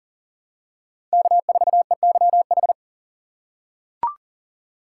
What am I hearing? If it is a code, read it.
Morse code, heard as a keyed tone, about 40 wpm, 700 Hz. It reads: K4EYH
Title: USA Call Signs - 2x Character Spacing - 40wpm